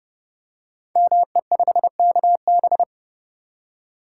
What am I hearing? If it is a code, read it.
Morse code, heard as a keyed tone, about 30 wpm, 700 Hz. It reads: ME5KB
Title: Code Groups - 5 Letter:Numbers - 30wpm